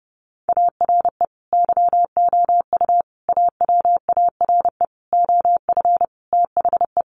Morse code, heard as a keyed tone, about 30 wpm, 700 Hz. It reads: ARE YOU AWARE OF THE
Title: Sentences from Top 1200 Words 30wpm